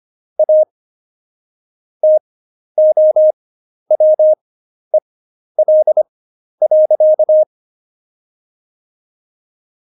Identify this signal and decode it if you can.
Morse code, heard as a keyed tone, about 25 wpm, spaced as for 11 wpm, 620 Hz. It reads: A TOWEL.
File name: Jokes-25-char-10-eff